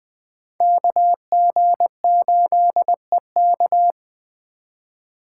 Morse code, heard as a keyed tone, about 20 wpm, 700 Hz. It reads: KG8EK
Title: Speed Racing - USA 2x2 Call Signs 17wpm